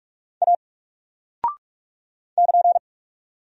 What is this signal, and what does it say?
Morse code, heard as a keyed tone, about 44 wpm, 700 Hz. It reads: A <KN>
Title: Single Character (Letter:Number:Punctuation:Prosign) 45wpm